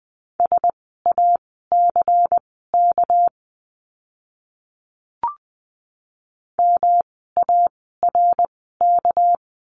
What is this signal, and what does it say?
Morse code, heard as a keyed tone, about 20 wpm, spaced as for 14 wpm, 700 Hz. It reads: SACK MARK
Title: 199 Lesson 32f - Letter k - Words 20wpm